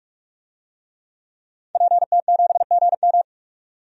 Morse code, heard as a keyed tone, about 47 wpm, 700 Hz. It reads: PT7GM
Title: DXCC-100 Call Signs - 5 Characters - 45wpm